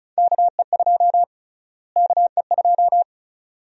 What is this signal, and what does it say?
Morse code, heard as a keyed tone, about 36 wpm, 700 Hz. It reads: KE2 KE2
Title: USA Call Sign Prefixes - Repeated 2x - 35wpm